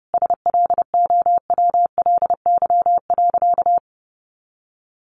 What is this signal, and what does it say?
Morse code, heard as a keyed tone, about 30 wpm, 700 Hz. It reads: SLOWLY.
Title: Additional Easy Phrases 30wpm